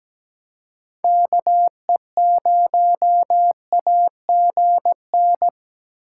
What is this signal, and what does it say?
Morse code, heard as a keyed tone, about 17 wpm, 700 Hz. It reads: KE0AGN
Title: USA Call Signs - 6 Characters - 3x Character Spacing - 17wpm